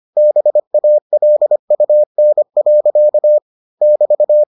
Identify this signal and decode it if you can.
Morse code, heard as a keyed tone, about 25 wpm, 600 Hz. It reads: BALUN. <BT>